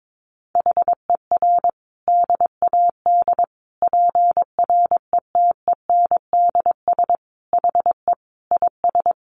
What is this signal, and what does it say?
Morse code, heard as a keyed tone, about 22 wpm, 700 Hz. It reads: HER DAD PRETENDS HE IS
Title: Additional Easy Phrases 22wpm